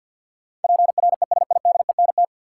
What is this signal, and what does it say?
Morse code, heard as a keyed tone, about 51 wpm, 700 Hz. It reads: PRESIDENT
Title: Sentences from Top 1000 Words - New Word Drill - 50wpm